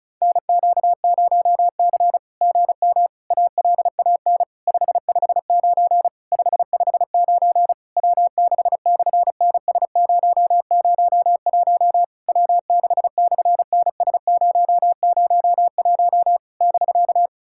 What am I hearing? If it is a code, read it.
Morse code, heard as a keyed tone, about 36 wpm, 700 Hz. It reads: NQ0C GM ALAN 559 559 W6/NS001 W6/NS001 <BK>
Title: SOTA QSO Practice 35wpm